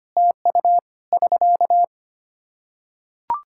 Morse code, heard as a keyed tone, about 25 wpm, 700 Hz. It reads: TU <SK>
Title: K1USN SST QSO Practice 25wpm